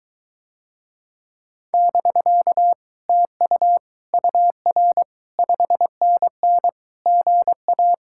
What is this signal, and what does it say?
Morse code, heard as a keyed tone, about 22 wpm, 700 Hz. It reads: <BK> TU UR 5NN GA